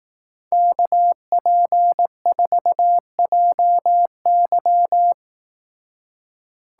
Morse code, heard as a keyed tone, about 18 wpm, 700 Hz. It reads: KP4JY
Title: Speed Racing - USA Call Signs 15wpm